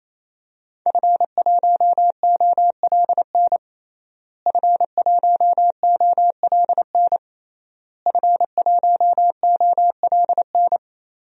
Morse code, heard as a keyed tone, about 28 wpm, 700 Hz. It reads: F1OLN F1OLN F1OLN
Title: DXCC-100 Call Signs - 5 Characters - Repeated 3x - 28wpm